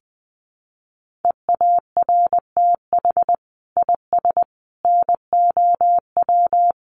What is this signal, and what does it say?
Morse code, heard as a keyed tone, about 19 wpm, 700 Hz. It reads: EARTH IS NOW
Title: Sentences from Top 600 Words 20wpm